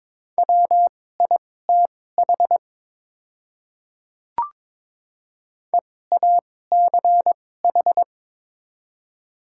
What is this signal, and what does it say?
Morse code, heard as a keyed tone, about 22 wpm, spaced as for 16 wpm, 700 Hz. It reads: WITH EACH
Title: 2x Character Spacing - Top 100 Words - 22wpm